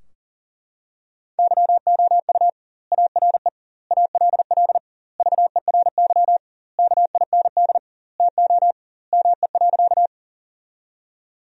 Morse code, heard as a keyed tone, about 40 wpm, 700 Hz. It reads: YOU ARE ALL VERY KIND TO ME.
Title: 2x Word-Spacing - Sentences from Top 200 Words - 40wpm